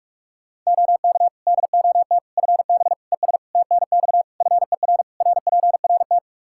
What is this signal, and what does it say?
Morse code, heard as a keyed tone, about 46 wpm, 700 Hz. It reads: OK DOT FB ES TNX FER RPRT
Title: Practice QSO - Exchange 2 - Info - 45wpm